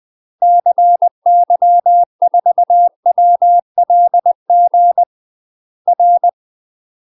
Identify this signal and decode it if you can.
Morse code, heard as a keyed tone, about 20 wpm, 700 Hz. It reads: CY4WLG R
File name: UK MASTER QSOs TRAINING/20WPM